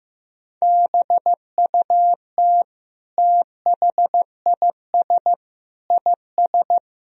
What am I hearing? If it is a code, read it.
Morse code, heard as a keyed tone, about 15 wpm, 700 Hz. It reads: BUT THIS IS
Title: Sentences from Top 1000 Words - Review Part 3 - 15wpm